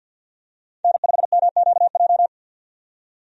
Morse code, heard as a keyed tone, about 50 wpm, 700 Hz. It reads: N5MQJ
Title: USA Call Signs - 5 Characters - 50wpm